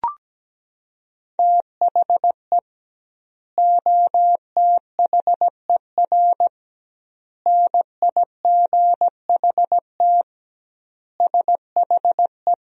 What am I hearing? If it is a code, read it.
Morse code, heard as a keyed tone, about 17 wpm, 700 Hz. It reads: THE OTHER NIGHT SHE
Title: 2x Word-Spacing - Sentences from Top 300 Words - 17wpm